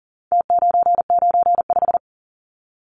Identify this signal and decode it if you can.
Morse code, heard as a keyed tone, about 40 wpm, 700 Hz. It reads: T995
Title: Numbers - Contest Style - 4-Digit 40wpm